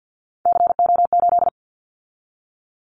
Morse code, heard as a keyed tone, about 48 wpm, 700 Hz. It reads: CO8
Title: Callsign Prefixes for Top 100 DXCC - 50wpm